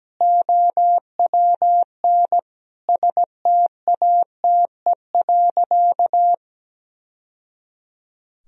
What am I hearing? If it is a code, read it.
Morse code, heard as a keyed tone, about 17 wpm, 700 Hz. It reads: OWN STATE.
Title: Sentences from Top 1000 Words - Review Part 3 - 17wpm